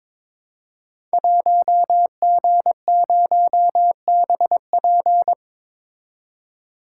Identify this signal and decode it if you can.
Morse code, heard as a keyed tone, about 22 wpm, 700 Hz. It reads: 1G0BP
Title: Code Groups - 5 Letter:Numbers - 22wpm